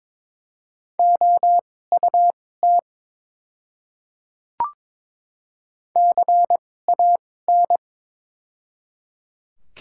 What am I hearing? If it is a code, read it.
Morse code, heard as a keyed tone, about 22 wpm, spaced as for 16 wpm, 700 Hz. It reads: OUT CAN
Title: 2x Character Spacing - Top 100 Words - 22wpm